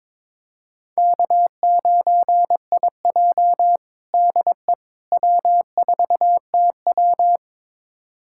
Morse code, heard as a keyed tone, about 22 wpm, 700 Hz. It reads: K9IJ DE W4TW
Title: USA Callsign DE Callsign - 4 Characters - 22wpm